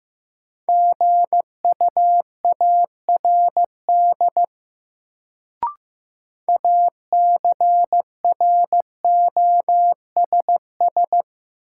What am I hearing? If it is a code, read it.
Morse code, heard as a keyed tone, about 15 wpm, 700 Hz. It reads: GUARD ACROSS
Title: Sentences from Top 700 Words - New Word Drill - 15wpm